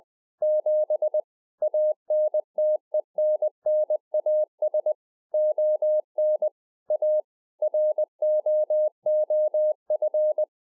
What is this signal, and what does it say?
Morse code, heard as a keyed tone, about 20 wpm, 610 Hz. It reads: 7 ANTENNAS ON A ROOF